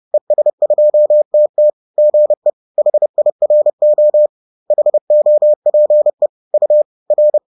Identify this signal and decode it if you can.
Morse code, heard as a keyed tone, about 30 wpm, 600 Hz. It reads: ES2TT GE HIRO HOPE U R